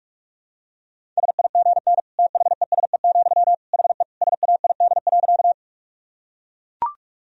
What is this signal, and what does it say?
Morse code, heard as a keyed tone, about 42 wpm, 700 Hz. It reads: SIGN THESE, HE SAID.
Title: Sentences from Top 1000 Words - Review Part 1 - 45wpm